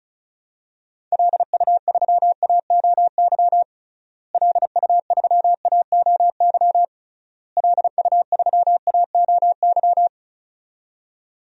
Morse code, heard as a keyed tone, about 36 wpm, 700 Hz. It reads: LU3AOY LU3AOY LU3AOY
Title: DXCC-100 Call Signs - 6 Characters - Repeated 3x - 35wpm